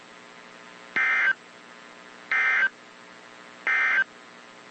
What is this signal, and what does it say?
Emergency Alert System (SAME) tones, heard as harsh buzzing bursts. It is heard from 0.8 to 4.1 s.
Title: WZ2523 - NOAA Weather Radio